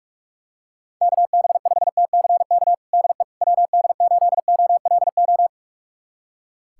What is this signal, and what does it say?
Morse code, heard as a keyed tone, about 43 wpm, 700 Hz. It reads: KB5TCK DE WD8OLO
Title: USA Callsign DE Callsign - 6 Characters - 45wpm